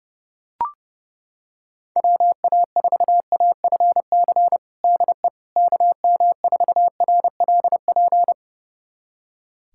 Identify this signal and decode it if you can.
Morse code, heard as a keyed tone, about 30 wpm, 700 Hz. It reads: WA4AFC DE KM4RLP
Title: USA Callsign DE Callsign - 6 Characters - 30wpm